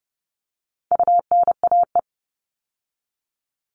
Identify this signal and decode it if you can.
Morse code, heard as a keyed tone, about 29 wpm, 700 Hz. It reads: UNAE
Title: All Cut Numbers - 4-Digit 30wpm